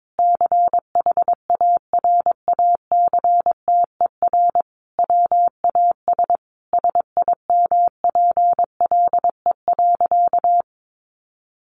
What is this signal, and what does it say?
Morse code, heard as a keyed tone, about 22 wpm, 700 Hz. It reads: CHARACTER WAS SIMPLE.
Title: Sentences from Top 1200 Words 22wpm